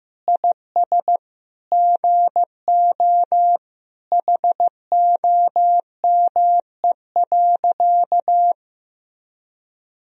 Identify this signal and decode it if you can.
Morse code, heard as a keyed tone, about 15 wpm, 700 Hz. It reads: IS GO HOME.
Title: Sentences from Top 200 Words and Abbreviations 15wpm